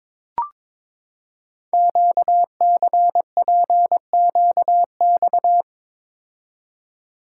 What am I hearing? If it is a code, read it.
Morse code, heard as a keyed tone, about 22 wpm, 700 Hz. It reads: QCPQX
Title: Code Groups - 5 Letters - 22wpm